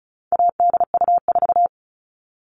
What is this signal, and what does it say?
Morse code, heard as a keyed tone, about 34 wpm, 700 Hz. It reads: ADU4
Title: All Cut Numbers - 4-Digit 35wpm